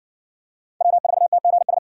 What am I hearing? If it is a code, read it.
Morse code, heard as a keyed tone, about 52 wpm, 700 Hz. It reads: W4TGS